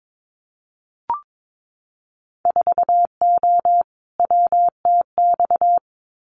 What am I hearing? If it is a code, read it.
Morse code, heard as a keyed tone, about 22 wpm, 700 Hz. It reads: 4O WTX
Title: Winter Field Day Exchanges Encore 22wpm